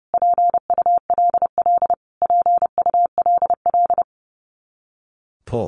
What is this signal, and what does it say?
Morse code, heard as a keyed tone, about 30 wpm, 700 Hz. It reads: PULL PULL
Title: Top 500 Words Repeated 30wpm